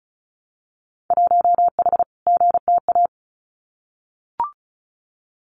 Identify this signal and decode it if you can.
Morse code, heard as a keyed tone, about 36 wpm, 700 Hz. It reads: 1H GTA
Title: Winter Field Day Exchanges 35wpm